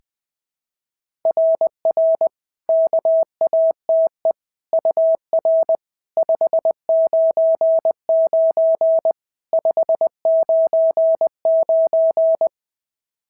Morse code, heard as a keyed tone, about 19 wpm, 650 Hz. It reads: RR KATE UR 599 599
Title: SOTA QSO Practice 20wpm